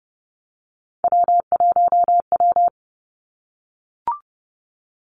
Morse code, heard as a keyed tone, about 30 wpm, 700 Hz. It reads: W1W
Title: Three Challenging Letter-Number Combinations 30wpm